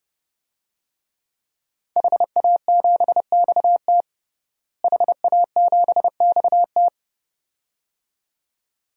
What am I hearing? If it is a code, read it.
Morse code, heard as a keyed tone, about 32 wpm, 700 Hz. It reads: HA7XT HA7XT
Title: DXCC-100 Call Signs - 5 Characters - Repeated 2x - 30wpm